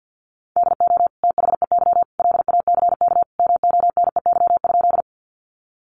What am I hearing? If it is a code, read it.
Morse code, heard as a keyed tone, about 50 wpm, 700 Hz. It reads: DO THEY LACK MONEY?
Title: Sentences from Top 1100 Words 50wpm